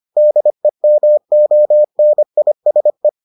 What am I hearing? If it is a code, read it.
Morse code, heard as a keyed tone, about 25 wpm, 600 Hz. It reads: DEMONISE